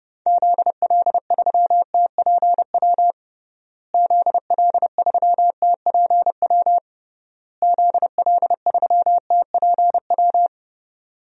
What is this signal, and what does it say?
Morse code, heard as a keyed tone, about 30 wpm, 700 Hz. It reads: ZL3TPW ZL3TPW ZL3TPW
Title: DXCC-100 Call Signs - 6 Characters - Repeated 3x - 30wpm